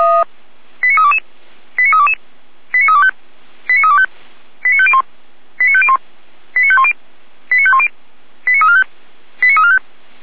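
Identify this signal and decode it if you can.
Five-tone selective call (ZVEI), heard as quick runs of five tones, tones of about 80 ms. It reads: 89322 89322 89325 89325 89501 89501 89510 89510 89345 89345
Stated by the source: ZVEI five-tone call 89230 received by SDR, recorded by feuerwehr-orlamuende.org fire brigade server.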